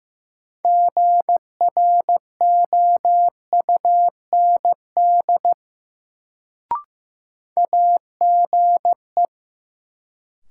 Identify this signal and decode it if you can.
Morse code, heard as a keyed tone, about 15 wpm, 700 Hz. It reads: GROUND AGE
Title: Sentences from Top 500 Words - New Word Drill - 15wpm